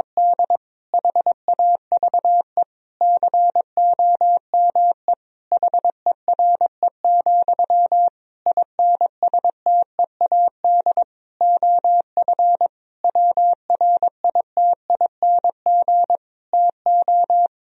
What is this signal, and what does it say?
Morse code, heard as a keyed tone, about 22 wpm, 700 Hz. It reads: D HAVE COME HERE, INSTEAD OF WRITING TO
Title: Sentences from Top 1000 Words - Review Part 4 - 22wpm